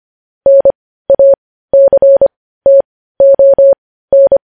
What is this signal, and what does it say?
Morse code, heard as a keyed tone, about 25 wpm, spaced as for 15 wpm, 550 Hz. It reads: NACTON
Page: news-headlines-in-morse-code-at-15-wpm